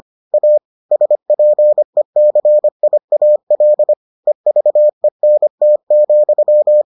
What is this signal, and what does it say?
Morse code, heard as a keyed tone, about 25 wpm, 600 Hz. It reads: A SPECIAL EVENT,